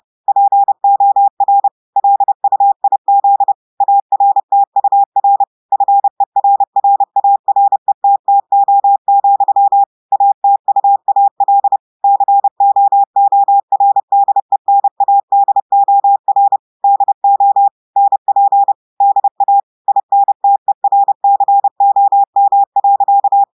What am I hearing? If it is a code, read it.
Morse code, heard as a keyed tone, about 30 wpm, 800 Hz. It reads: POR LUIZ ARTUR FERRARETTO, ATUAL COORDENADOR DO NP DA INTERCOM.
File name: audios / portugais / portugais texte en clair 1 vitesse 30 mots minutes